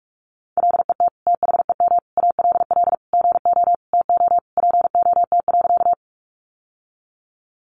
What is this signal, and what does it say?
Morse code, heard as a keyed tone, about 44 wpm, 700 Hz. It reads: LET THEM ALL GO TO POT.
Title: Sentences from Top 1000 Words 45wpm